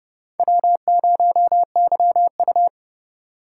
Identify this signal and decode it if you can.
Morse code, heard as a keyed tone, about 30 wpm, 700 Hz. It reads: W0YU
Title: USA Call Signs - 4x Character Spacing - 30wpm